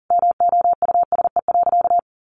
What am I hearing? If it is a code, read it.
Morse code, heard as a keyed tone, about 40 wpm, 700 Hz. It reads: MOUSE.
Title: A Sound Of Thunder 40wpm